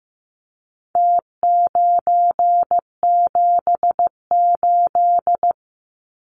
Morse code, heard as a keyed tone, about 15 wpm, 700 Hz. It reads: T978
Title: Numbers - Contest Style - 4-Digit 15wpm